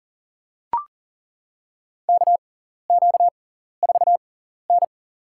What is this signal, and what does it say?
Morse code, heard as a keyed tone, about 40 wpm, spaced as for 14 wpm, 700 Hz. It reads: KQ4N
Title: USA Call Signs - 4 Characters - 6x Character Spacing - 40wpm